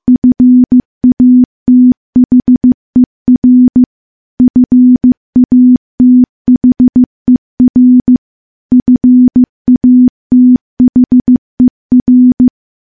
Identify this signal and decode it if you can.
Morse code, heard as a keyed tone, about 15 wpm, 250 Hz. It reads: FATHER FATHER FATHER
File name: morse / morse-training-15WPM